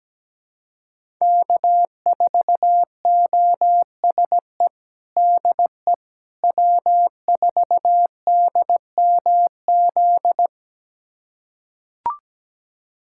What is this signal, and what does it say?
Morse code, heard as a keyed tone, about 17 wpm, 700 Hz. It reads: K4OSE DE W4DMZ
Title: USA Callsign DE Callsign - 5 Characters - 17wpm